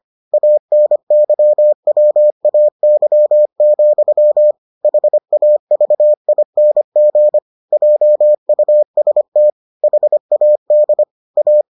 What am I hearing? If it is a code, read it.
Morse code, heard as a keyed tone, about 25 wpm, 600 Hz. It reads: ANYWAY, HAVING JUST HAD A